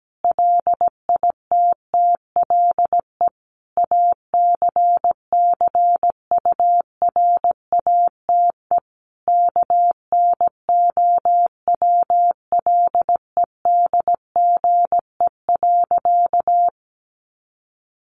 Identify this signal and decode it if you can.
Morse code, heard as a keyed tone, about 17 wpm, 700 Hz. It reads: LITTLE ACCURATE KNOWLEDGE.
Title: Sentences from Top 1200 Words 17wpm